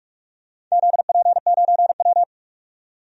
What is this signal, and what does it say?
Morse code, heard as a keyed tone, about 45 wpm, 700 Hz. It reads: ZP9W
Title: DXCC-100 Call Signs - 4 Characters - 45wpm